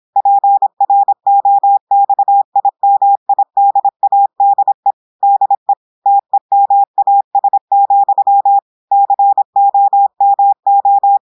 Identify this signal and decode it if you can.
Morse code, heard as a keyed tone, about 26 wpm, 800 Hz. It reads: PROXIMIDADE DE TEMAS, COMO
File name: audios / portugais / portugais texte en clair 1 vitesse 26 mots minutes